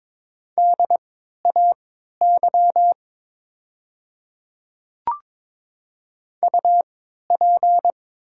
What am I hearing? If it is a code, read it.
Morse code, heard as a keyed tone, about 22 wpm, spaced as for 12 wpm, 700 Hz. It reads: DAY UP
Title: 3x Character Spacing - Top 100 Words - 22wpm